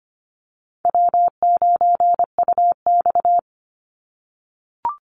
Morse code, heard as a keyed tone, about 25 wpm, 700 Hz. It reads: W9UX
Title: NA SOTA Chaser Callsigns 25wpm